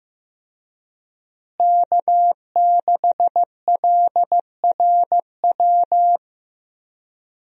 Morse code, heard as a keyed tone, about 15 wpm, 700 Hz. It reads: K6LRW
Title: USA Call Signs - 5 Characters - Repeated 2x - 15wpm